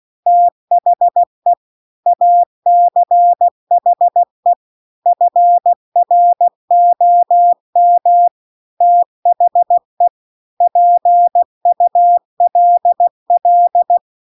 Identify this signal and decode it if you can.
Morse code, heard as a keyed tone, about 16 wpm, 700 Hz. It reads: THE ACHE FROM THE PULL